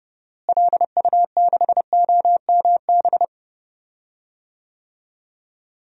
Morse code, heard as a keyed tone, about 31 wpm, 700 Hz. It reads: LU6OMB
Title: DXCC-100 Call Signs - 6 Characters - Repeated 3x - 30wpm